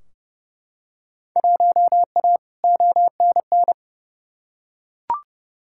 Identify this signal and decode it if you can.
Morse code, heard as a keyed tone, about 30 wpm, 700 Hz. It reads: 1A ONN
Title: Field Day Exchanges Encore - 30wpm